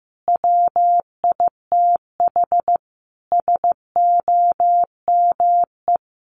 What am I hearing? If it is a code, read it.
Morse code, heard as a keyed tone, about 14 wpm, 700 Hz. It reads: WITH SOME
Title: Sentences from Top 1500 Words 15wpm